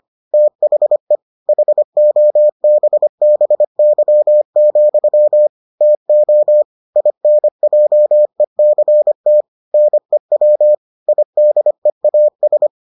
Morse code, heard as a keyed tone, about 25 wpm, 600 Hz. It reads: THE HOBBY, TO INJECT NEW IDEAS